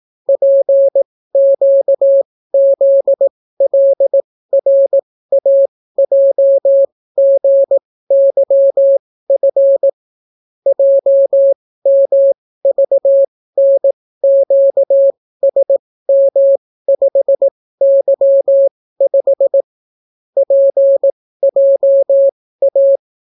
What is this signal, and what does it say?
Morse code, heard as a keyed tone, about 18 wpm, spaced as for 15 wpm, 550 Hz. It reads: PQZLRAJGYF JMVNQSM5Y5 PJA